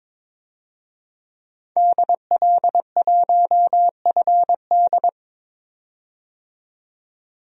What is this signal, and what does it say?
Morse code, heard as a keyed tone, about 21 wpm, 700 Hz. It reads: DL1FD
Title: DXCC-100 Call Signs - 5 Characters - 22wpm